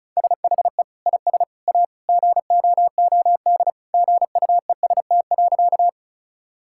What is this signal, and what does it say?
Morse code, heard as a keyed tone, about 36 wpm, 700 Hz. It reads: SHE IS A GOOD GUEST.
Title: Sentences from Top 1600 Words 35wpm